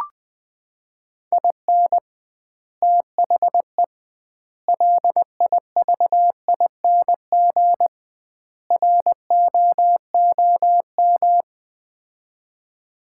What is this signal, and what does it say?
Morse code, heard as a keyed tone, about 20 wpm, 700 Hz. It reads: IN THE LIVING ROOM
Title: 2x Word-Spacing - Sets of 4 Words - 20wpm